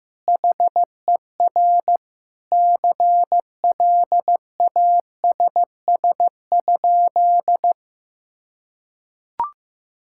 Morse code, heard as a keyed tone, about 15 wpm, 700 Hz. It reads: HER CLASS?
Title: Sentences from Top 1000 Words - Review Part 6 - 15wpm